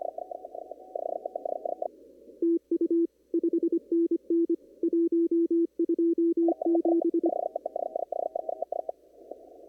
Morse code, heard as a keyed tone, about 22 wpm, 340 Hz. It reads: TU 5NN 127
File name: CW / kt5j wpx cw-op-k5pi